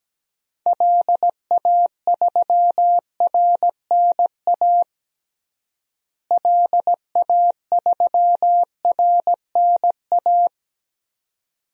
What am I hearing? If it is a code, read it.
Morse code, heard as a keyed tone, about 17 wpm, 700 Hz. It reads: LA3RNA LA3RNA
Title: DXCC-100 Call Signs - 6 Characters - Repeated 2x - 17wpm